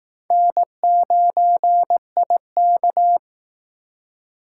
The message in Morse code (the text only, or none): N9IK